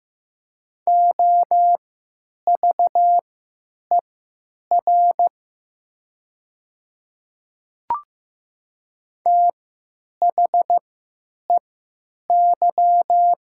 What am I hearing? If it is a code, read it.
Morse code, heard as a keyed tone, about 15 wpm, spaced as for 9 wpm, 700 Hz. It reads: OVER THEY
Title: 3x Character Spacing - Top 100 Words - 15wpm